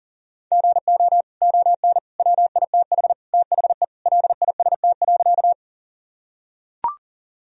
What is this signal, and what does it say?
Morse code, heard as a keyed tone, about 37 wpm, 700 Hz. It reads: GO ON WITH THE LIST.
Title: Sentences from Top 1000 Words - Review Part 2 - 40wpm